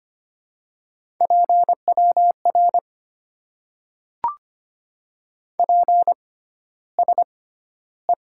Morse code, heard as a keyed tone, about 25 wpm, spaced as for 13 wpm, 700 Hz. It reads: PWR PSE
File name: Top 90 QSO Elements - 6x Character Spacing - 25wpm